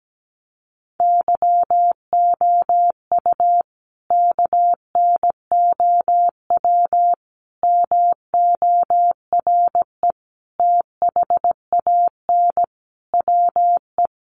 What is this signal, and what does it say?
Morse code, heard as a keyed tone, about 17 wpm, 700 Hz. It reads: YOU KNOW MORE THAN WE
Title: Sentences from Top 100 Words 17wpm